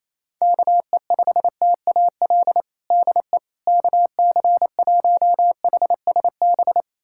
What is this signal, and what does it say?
Morse code, heard as a keyed tone, about 28 wpm, 700 Hz. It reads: KE5TAL DE KC1HSB
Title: USA Callsign DE Callsign - 6 Characters - 28wpm